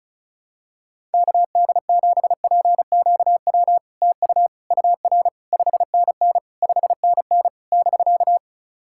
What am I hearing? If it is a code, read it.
Morse code, heard as a keyed tone, about 36 wpm, 690 Hz. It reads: KD7PQW TU UR 5NN 5NN <BK>
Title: SOTA QSO Practice 35wpm